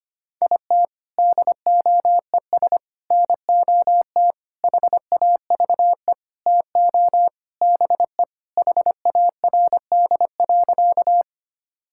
Morse code, heard as a keyed tone, about 25 wpm, 700 Hz. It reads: IT DOES NOT HAVE TO BE HARD.
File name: practice / 25wpm / Sentences with 4-Word Phrases - 25wpm